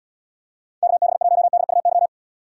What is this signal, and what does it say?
Morse code, heard as a keyed tone, about 79 wpm, 700 Hz. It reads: KB1DUY